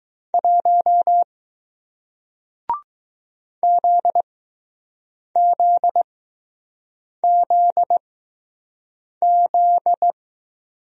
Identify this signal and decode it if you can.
Morse code, heard as a keyed tone, about 20 wpm, spaced as for 6 wpm, 700 Hz. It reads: 1 ZZZZ